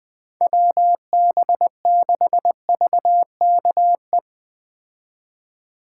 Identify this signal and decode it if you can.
Morse code, heard as a keyed tone, about 20 wpm, 700 Hz. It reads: WB6VKE
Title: Speed Racing - USA 2x3 Call Signs 17wpm